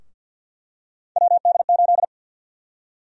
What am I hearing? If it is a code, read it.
Morse code, heard as a keyed tone, about 52 wpm, 700 Hz. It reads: WD8
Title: USA Call Sign Prefixes - 50wpm